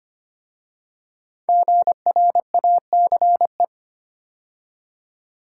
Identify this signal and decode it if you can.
Morse code, heard as a keyed tone, about 25 wpm, 700 Hz. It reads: GRACE
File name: 5 Letter Words - First 3 Letters Given - 25wpm